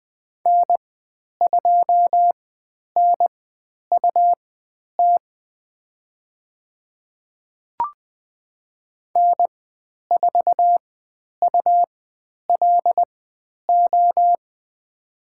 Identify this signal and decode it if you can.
Morse code, heard as a keyed tone, about 20 wpm, spaced as for 10 wpm, 700 Hz. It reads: N2NUT N4ULO
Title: 102 Lesson 17g - Letter u - Call Signs 20wpm